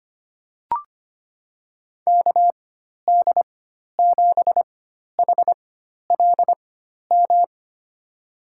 Morse code, heard as a keyed tone, about 25 wpm, spaced as for 12 wpm, 700 Hz. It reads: KD7HLM
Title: USA Call Signs - 6 Characters - 4x Character Spacing - 25wpm